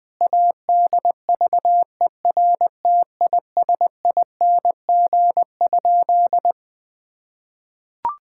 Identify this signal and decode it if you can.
Morse code, heard as a keyed tone, about 20 wpm, 700 Hz. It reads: ADVERTISING?
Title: Sentences from Top 1000 Words - Review Part 6 - 20wpm